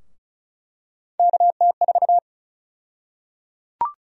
Morse code, heard as a keyed tone, about 34 wpm, 700 Hz. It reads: KT4
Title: USA Call Sign Prefixes - 35wpm